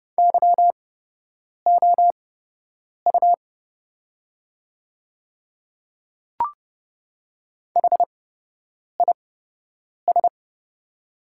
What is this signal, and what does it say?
Morse code, heard as a keyed tone, about 30 wpm, spaced as for 8 wpm, 700 Hz. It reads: YOU HIS
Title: 8x Character Spacing - Top 100 Words - 30wpm